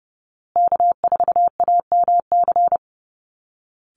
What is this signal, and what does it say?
Morse code, heard as a keyed tone, about 30 wpm, 700 Hz. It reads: K4AMC
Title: Call Signs Repeated 2x - 30wpm